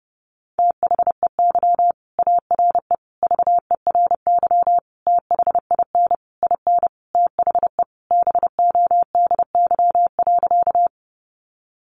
Morse code, heard as a keyed tone, about 30 wpm, 700 Hz. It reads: THEY ARE VERY THIN IN THE BODY.
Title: Sentences from Top 1200 Words 30wpm